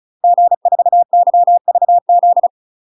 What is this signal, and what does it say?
Morse code, heard as a keyed tone, about 36 wpm, 700 Hz. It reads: G4YVZ
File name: UK MASTER QSOs TRAINING / 35WPM / UK MASTER QSOs 0022